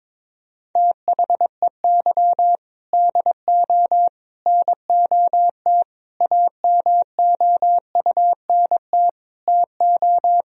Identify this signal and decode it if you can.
Morse code, heard as a keyed tone, about 22 wpm, 700 Hz. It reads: THEY DO NOT AMOUNT TO
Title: Sentences from Top 1000 Words - Review Part 4 - 22wpm